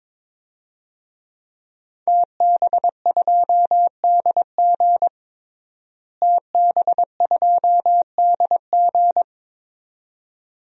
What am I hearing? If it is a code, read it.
Morse code, heard as a keyed tone, about 22 wpm, 700 Hz. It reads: TB2DG TB2DG
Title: DXCC-100 Call Signs - 5 Characters - Repeated 2x - 22wpm